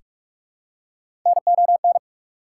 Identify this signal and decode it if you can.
Morse code, heard as a keyed tone, about 44 wpm, 690 Hz. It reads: NON
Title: Prefixes and Suffixes 45wpm